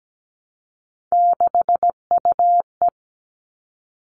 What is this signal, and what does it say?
Morse code, heard as a keyed tone, about 17 wpm, 700 Hz. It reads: 6UE
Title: All Cut Numbers - 3-Digit 17wpm